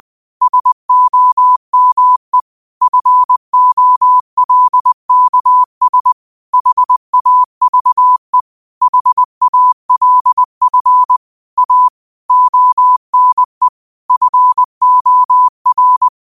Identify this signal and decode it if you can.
Morse code, heard as a keyed tone, about 20 wpm, 1000 Hz. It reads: SOME FOLKS HAVE HALF A ONE FOR
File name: qotdfiles / 20wpm